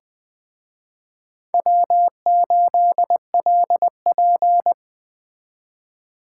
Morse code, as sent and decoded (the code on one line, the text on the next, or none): .-- ---.. .-.. .--.
W8LP